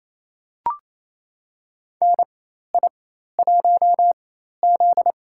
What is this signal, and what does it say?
Morse code, heard as a keyed tone, about 28 wpm, spaced as for 13 wpm, 700 Hz. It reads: NI1Z
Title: USA Call Signs - 4x Character Spacing - 28wpm